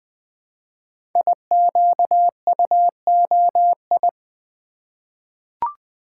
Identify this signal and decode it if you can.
Morse code, heard as a keyed tone, about 20 wpm, 700 Hz. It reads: IQUOI
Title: Code Groups - 5 Letters - 20wpm